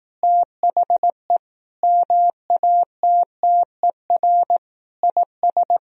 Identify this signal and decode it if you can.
Morse code, heard as a keyed tone, about 18 wpm, 700 Hz. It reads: THE MATTER IS
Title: Speed-Racing - Sets of 5 Words 15wpm